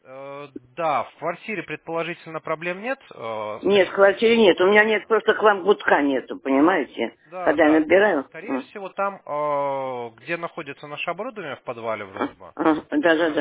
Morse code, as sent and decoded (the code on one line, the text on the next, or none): none